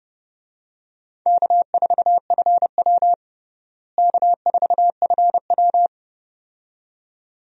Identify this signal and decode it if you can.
Morse code, heard as a keyed tone, about 29 wpm, 700 Hz. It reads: K4FW K4FW
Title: USA Call Signs - 4 Characters - Repeated 2x - 30wpm